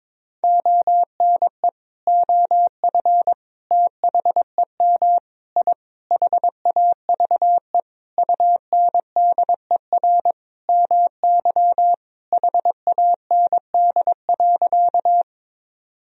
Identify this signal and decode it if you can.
Morse code, heard as a keyed tone, about 22 wpm, 700 Hz. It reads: ONE OF THEM I HAVE UNDER MY HAND.
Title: Sentences from Top 1000 Words - Review Part 1 - 22wpm